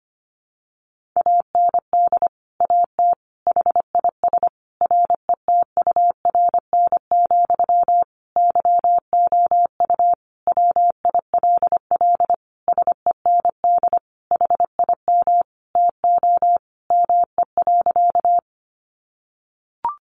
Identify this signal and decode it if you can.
Morse code, heard as a keyed tone, about 25 wpm, 700 Hz. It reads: AND AT HIS RETURN, YOU WILL SEND HIM TO ME.
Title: Sentences from Top 1500 Words 25wpm